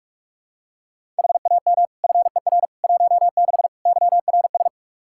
Morse code, heard as a keyed tone, about 46 wpm, 700 Hz. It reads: HAM FER 16 YRS
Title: Practice QSO - Exchange 3 - More Info - 45wpm